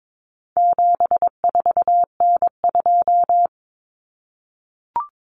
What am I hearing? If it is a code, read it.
Morse code, heard as a keyed tone, about 22 wpm, 700 Hz. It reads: 74N2
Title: Common Cut Numbers - 4-Digit 22wpm